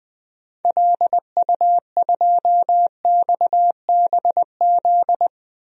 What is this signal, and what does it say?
Morse code, heard as a keyed tone, about 21 wpm, 700 Hz. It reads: LU2XBZ
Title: DXCC-100 Call Signs - 6 Characters - 20wpm